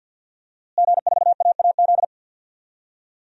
Morse code, heard as a keyed tone, about 51 wpm, 700 Hz. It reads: G4AAZ
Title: DXCC-100 Call Signs - 5 Characters - Repeated 2x - 50wpm